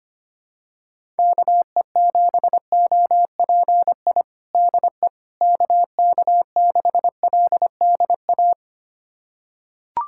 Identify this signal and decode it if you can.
Morse code, heard as a keyed tone, about 25 wpm, 700 Hz. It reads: KE7OPI DE KK6LDA
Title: USA Callsign DE Callsign - 6 Characters - 25wpm